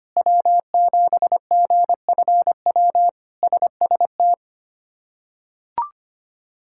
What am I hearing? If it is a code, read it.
Morse code, heard as a keyed tone, about 25 wpm, 700 Hz. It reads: W7GFW SST
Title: K1USN SST QSO Practice - 25wpm